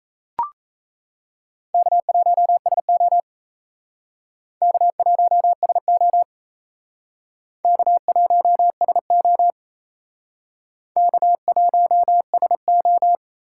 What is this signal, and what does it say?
Morse code, heard as a keyed tone, about 34 wpm, 700 Hz. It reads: K1SO K1SO K1SO K1SO
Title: Speed Racing - USA 1x2 Call Signs 28wpm